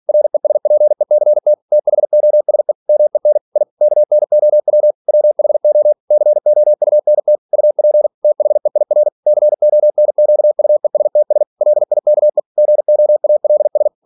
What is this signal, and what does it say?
Morse code, heard as a keyed tone, about 46 wpm, 600 Hz. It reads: RESPECT THOSE MEN I KNOW WHO COUNT UP THEIR CONQUESTS LIKE GOALS
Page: Never 47wpm